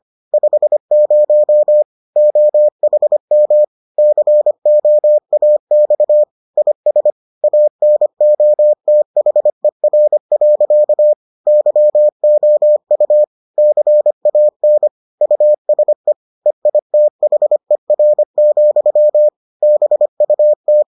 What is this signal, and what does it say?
Morse code, heard as a keyed tone, about 25 wpm, 600 Hz. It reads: 50 OHM COAX IS ANOTHER. YOU CAN USE EITHER, BUT